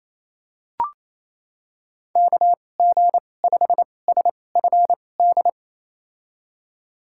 Morse code, heard as a keyed tone, about 28 wpm, spaced as for 20 wpm, 690 Hz. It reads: KG5SFD
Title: USA Call Signs - 6 Characters - 2x Character Spacing - 28wpm